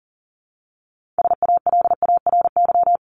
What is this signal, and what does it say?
Morse code, heard as a keyed tone, about 40 wpm, 700 Hz. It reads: SALARY
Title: Top 1751-2000 Words 40wpm